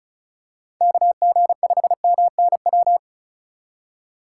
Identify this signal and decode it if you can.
Morse code, heard as a keyed tone, about 36 wpm, 700 Hz. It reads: KG5MNW
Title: USA Call Signs - 6 Characters - 4x Character Spacing - 35wpm